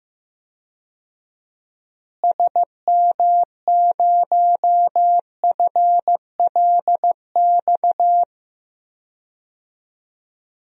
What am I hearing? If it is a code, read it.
Morse code, heard as a keyed tone, about 15 wpm, 700 Hz. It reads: SM0FLX
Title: DXCC-100 Call Signs - 6 Characters - Repeated 3x - 15wpm